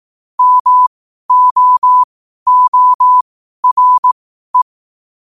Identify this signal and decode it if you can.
Morse code, heard as a keyed tone, about 18 wpm, spaced as for 12 wpm, 1000 Hz. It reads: MOORE
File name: qotdfiles / 12wpm